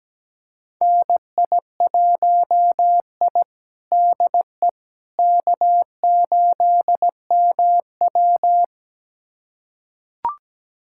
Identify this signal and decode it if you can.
Morse code, heard as a keyed tone, about 17 wpm, 700 Hz. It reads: NI1I DE K8MW
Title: USA Callsign DE Callsign - 4 Characters - 17wpm